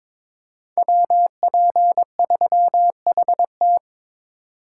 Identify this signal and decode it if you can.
Morse code, heard as a keyed tone, about 26 wpm, 700 Hz. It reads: WP3HT
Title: Speed Racing - USA 2x2 Call Signs 22wpm